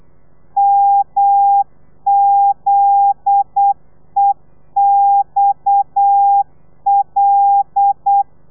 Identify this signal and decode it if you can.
Morse code, heard as a keyed tone, about 8 wpm, 800 Hz. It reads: MZEXL
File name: telegrafia / i2tfj / Lez9 35